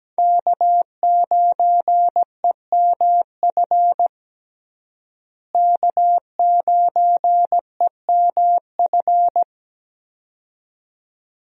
Morse code, heard as a keyed tone, about 17 wpm, 700 Hz. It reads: K9EMF K9EMF
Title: USA Call Signs - 5 Characters - Repeated 2x - 17wpm